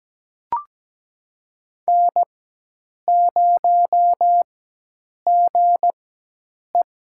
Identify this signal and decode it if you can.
Morse code, heard as a keyed tone, about 17 wpm, spaced as for 8 wpm, 700 Hz. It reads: N0GE
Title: USA Call Signs - 4 Characters - 4x Character Spacing - 17wpm